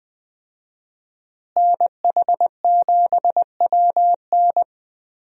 Morse code, heard as a keyed tone, about 20 wpm, 700 Hz. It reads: NH7WN
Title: USA Call Signs - 5 Characters - Repeated 2x - 20wpm